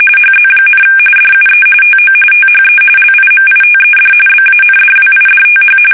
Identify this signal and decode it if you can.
RTTY, about 45.45 baud. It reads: HZ, DUE TO THE LIMITED BANDWIDTH
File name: fsk850 / teletype-RTTY